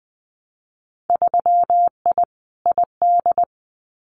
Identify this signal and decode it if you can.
Morse code, heard as a keyed tone, about 20 wpm, 700 Hz. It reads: 3I ID
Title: Winter Field Day Exchanges 20wpm